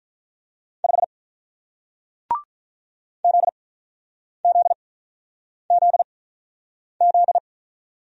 Morse code, heard as a keyed tone, about 45 wpm, spaced as for 9 wpm, 700 Hz. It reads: 5 ZZZZ